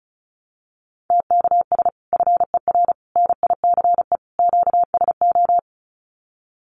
Morse code, heard as a keyed tone, about 34 wpm, 700 Hz. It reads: TKS FER NICE QSO
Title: Common QSO Phrases 35wpm